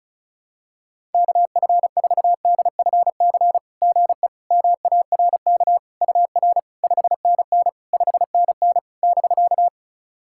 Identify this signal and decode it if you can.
Morse code, heard as a keyed tone, about 36 wpm, 690 Hz. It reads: KF4DFC GE MARK UR 5NN 5NN <BK>
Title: SOTA QSO Practice 35wpm